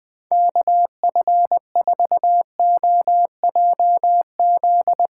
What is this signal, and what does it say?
Morse code, heard as a keyed tone, about 20 wpm, 700 Hz. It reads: KF4OJZ
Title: Speed Racing - USA Call Signs 17wpm